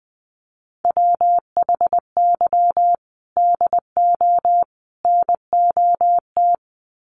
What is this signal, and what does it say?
Morse code, heard as a keyed tone, about 20 wpm, 700 Hz. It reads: WHY DO NOT
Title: Sets of 3 Words 20wpm